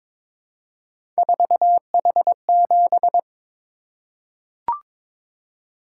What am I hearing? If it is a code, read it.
Morse code, heard as a keyed tone, about 22 wpm, 700 Hz. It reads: 4H7
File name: practice / 22wpm / Three Letter-Number Combinations - 22wpm